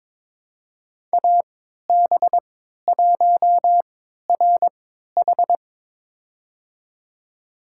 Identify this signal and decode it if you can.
Morse code, heard as a keyed tone, about 22 wpm, spaced as for 12 wpm, 700 Hz. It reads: AB1RH
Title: USA Call Signs - 5 Characters - 3x Character Spacing - 22wpm